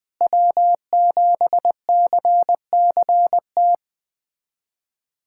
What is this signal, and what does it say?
Morse code, heard as a keyed tone, about 20 wpm, 700 Hz. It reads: W7CCT
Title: USA Callsign DE Callsign - 5 Characters - 20wpm